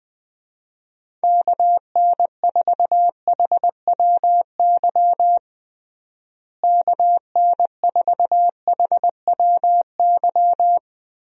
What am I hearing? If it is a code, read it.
Morse code, heard as a keyed tone, about 21 wpm, 700 Hz. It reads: KN4HWY KN4HWY
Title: USA Call Signs - 6 Characters - Repeated 3x - 20wpm